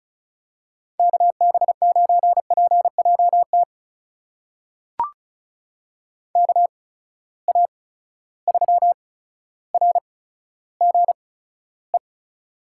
Morse code, heard as a keyed tone, about 36 wpm, spaced as for 15 wpm, 700 Hz. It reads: KB9PJT KA3RGE
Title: USA Call Signs - 6 Characters - 8x Character Spacing - 35wpm